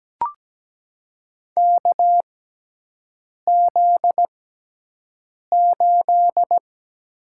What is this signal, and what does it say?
Morse code, heard as a keyed tone, about 17 wpm, spaced as for 6 wpm, 700 Hz. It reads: KZ8
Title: USA Call Signs - 6x Character Spacing - 17wpm